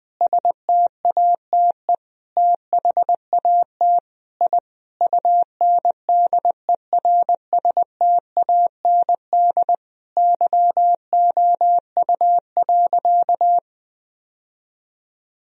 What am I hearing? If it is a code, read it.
Morse code, heard as a keyed tone, about 20 wpm, 700 Hz. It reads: STATE THAT I UNDERSTAND YOU.
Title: Sentences from Top 1000 Words - Review Part 2 - 20wpm